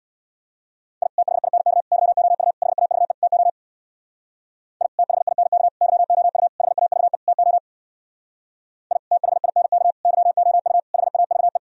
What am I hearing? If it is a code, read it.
Morse code, heard as a keyed tone, about 62 wpm, 700 Hz. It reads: I THINK YOU HAVE TO I THINK YOU HAVE TO I THINK YOU HAVE
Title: Speed-Racing - Sets of 5 Words 50wpm